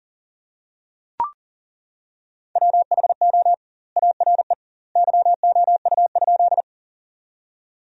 Morse code, heard as a keyed tone, about 40 wpm, 700 Hz. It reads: WHO ARE YOU?
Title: 2x Word-Spacing - Sentences from Top 100 Words - 40wpm